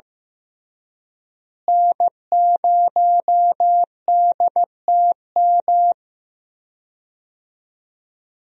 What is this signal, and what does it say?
Morse code, heard as a keyed tone, about 15 wpm, 700 Hz. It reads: N0DTM
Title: USA Call Signs - 5 Characters - Repeated 3x - 15wpm